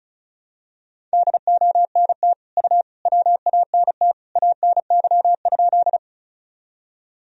Morse code, heard as a keyed tone, about 36 wpm, 700 Hz. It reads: DONT U WANT ANY?
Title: Sentences from Top 300 Words and Abbreviations 35wpm